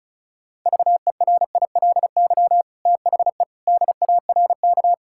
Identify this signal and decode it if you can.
Morse code, heard as a keyed tone, about 36 wpm, 700 Hz. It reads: VERILY THE DARK
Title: Pangram Challenge v2 - 35wpm